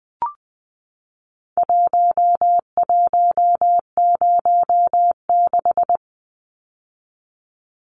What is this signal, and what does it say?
Morse code, heard as a keyed tone, about 20 wpm, 700 Hz. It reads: 1106
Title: Numbers - Contest Style - 4-Digit 20wpm